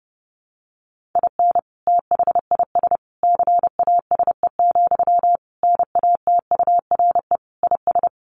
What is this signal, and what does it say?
Morse code, heard as a keyed tone, about 30 wpm, 700 Hz. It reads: IN THIS CASE, NATURE IS
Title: Sentences from Top 1400 Words 30wpm